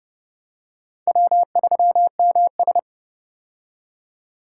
Morse code, heard as a keyed tone, about 30 wpm, 700 Hz. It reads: W3MS
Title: K1USN SST QSO Practice - 30wpm